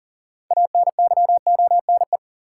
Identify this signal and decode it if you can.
Morse code, heard as a keyed tone, about 40 wpm, 700 Hz. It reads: ANYONE
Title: Sentences from Top 900 Words - New Word Drill - 40wpm